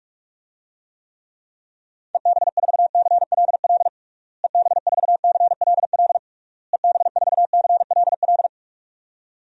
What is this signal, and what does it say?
Morse code, heard as a keyed tone, about 46 wpm, 700 Hz. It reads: EB4CLL EB4CLL EB4CLL
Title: DXCC-100 Call Signs - 6 Characters - Repeated 3x - 45wpm